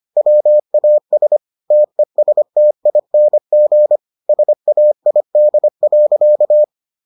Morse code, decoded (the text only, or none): WAS TESTING SAID.